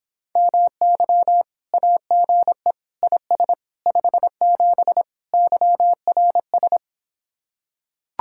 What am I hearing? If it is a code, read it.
Morse code, heard as a keyed tone, about 26 wpm, 700 Hz. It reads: MY AGE IS 57 YRS